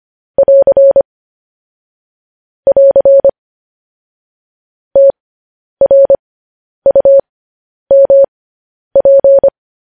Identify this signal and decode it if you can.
Morse code, heard as a keyed tone, about 25 wpm, spaced as for 10 wpm, 550 Hz. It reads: <AR> <AR> TRUMP
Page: news-headlines-in-morse-code-at-10-wpm